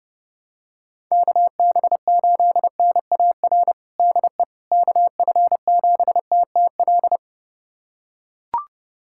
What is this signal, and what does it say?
Morse code, heard as a keyed tone, about 30 wpm, 700 Hz. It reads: KB8NAR DE KF7TTL
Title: USA Callsign DE Callsign - 6 Characters - 30wpm